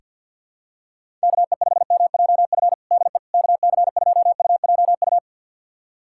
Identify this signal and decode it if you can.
Morse code, heard as a keyed tone, about 48 wpm, 700 Hz. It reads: KE5MJF DE KC2UJU
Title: USA Callsign DE Callsign - 6 Characters - 50wpm